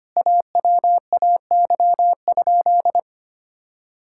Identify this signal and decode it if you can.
Morse code, heard as a keyed tone, about 25 wpm, 700 Hz. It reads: AWAY?
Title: Sentences from Top 1000 Words - Review Part 6 - 25wpm